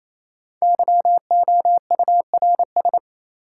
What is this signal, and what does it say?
Morse code, heard as a keyed tone, about 28 wpm, 700 Hz. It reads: YOURS
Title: Sentences from Top 1500 Words - New Word Drill - 28wpm